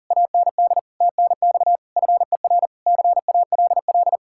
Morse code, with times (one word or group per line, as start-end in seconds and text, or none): 0.1-0.8 AND
1.0-1.8 TNX
2.0-2.7 FER
2.9-4.2 CALL